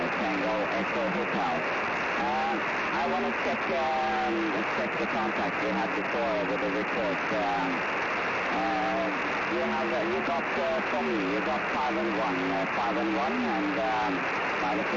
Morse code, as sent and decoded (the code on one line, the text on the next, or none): none